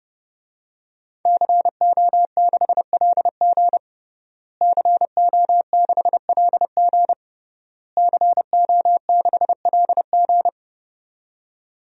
Morse code, heard as a keyed tone, about 30 wpm, 700 Hz. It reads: CO6LG CO6LG CO6LG
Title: DXCC-100 Call Signs - 5 Characters - Repeated 3x - 30wpm